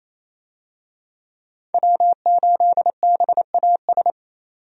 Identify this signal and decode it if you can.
Morse code, heard as a keyed tone, about 27 wpm, 700 Hz. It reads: W8BAS
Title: USA Call Signs - 5 Characters - 28wpm